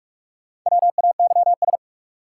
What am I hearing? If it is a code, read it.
Morse code, heard as a keyed tone, about 46 wpm, 700 Hz. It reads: WAYS
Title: Sets of 5 Words Encore - New Word Drill - 45wpm